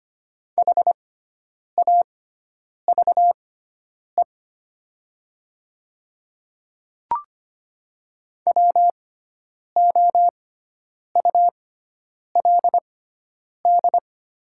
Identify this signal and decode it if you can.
Morse code, heard as a keyed tone, about 25 wpm, spaced as for 9 wpm, 700 Hz. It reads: HAVE WOULD
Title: 6x Character Spacing - Top 200 Words - 25wpm